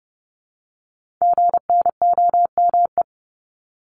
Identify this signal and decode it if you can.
Morse code, heard as a keyed tone, about 30 wpm, 700 Hz. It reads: GNOME
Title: Multi-letter Phonemes Repeated 3x 30wpm